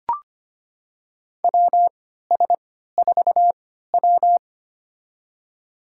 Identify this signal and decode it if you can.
Morse code, heard as a keyed tone, about 25 wpm, spaced as for 14 wpm, 700 Hz. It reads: WS4W
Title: USA Call Signs - 3x Character Spacing - 25wpm